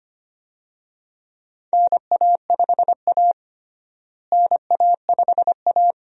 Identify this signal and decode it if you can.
Morse code, heard as a keyed tone, about 25 wpm, 700 Hz. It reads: NA5A NA5A
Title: USA Call Signs - 4 Characters - Repeated 2x - 25wpm